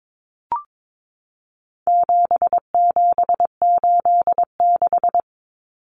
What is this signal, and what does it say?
Morse code, heard as a keyed tone, about 22 wpm, 700 Hz. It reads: 7786
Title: Numbers - 4-Digit 22wpm